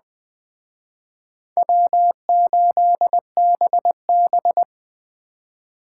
Morse code, heard as a keyed tone, about 20 wpm, 700 Hz. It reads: W8BB